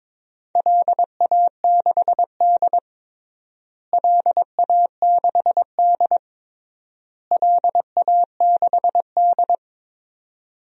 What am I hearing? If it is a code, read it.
Morse code, heard as a keyed tone, about 22 wpm, 700 Hz. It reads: LA6D LA6D LA6D
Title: DXCC-100 Call Signs - 4 Characters - Repeated 3x - 22wpm